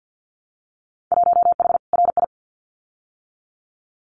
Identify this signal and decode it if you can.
Morse code, heard as a keyed tone, about 50 wpm, 700 Hz. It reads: 1H RI
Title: Winter Field Day Exchanges 50wpm